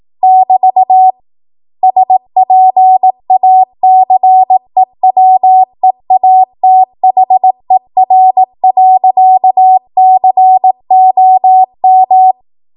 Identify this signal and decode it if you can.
Morse code, heard as a keyed tone, about 18 wpm, 750 Hz. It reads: <BT> SPACEWEATHER.COM
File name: Morse / 18WPM